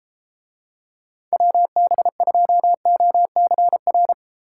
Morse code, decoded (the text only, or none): WB2OCR